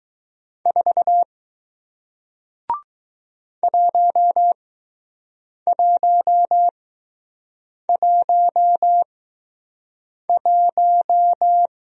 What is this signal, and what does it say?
Morse code, heard as a keyed tone, about 20 wpm, spaced as for 6 wpm, 700 Hz. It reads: 4 1111